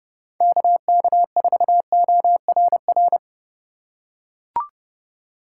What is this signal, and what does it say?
Morse code, heard as a keyed tone, about 30 wpm, 700 Hz. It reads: KK4ORR
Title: USA Call Signs - 6 Characters - 8x Character Spacing - 30wpm